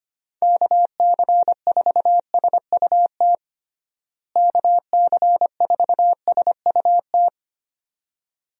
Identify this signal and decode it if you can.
Morse code, heard as a keyed tone, about 25 wpm, 700 Hz. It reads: KC4SUT KC4SUT
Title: USA Call Signs - 6 Characters - Repeated 2x - 25wpm